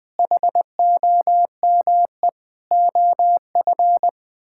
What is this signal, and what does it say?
Morse code, heard as a keyed tone, about 20 wpm, 700 Hz. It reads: HOME OF
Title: Sets of 9 Words - 20wpm